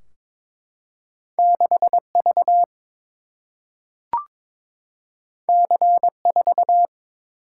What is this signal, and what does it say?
Morse code, heard as a keyed tone, about 22 wpm, 700 Hz. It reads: 6V C4
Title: Single-Letter Single-Number 22wpm